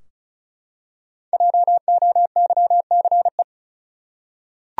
Morse code, heard as a keyed tone, about 34 wpm, 700 Hz. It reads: JOYCE
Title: USA Common Names - Max 5 Letters - 35wpm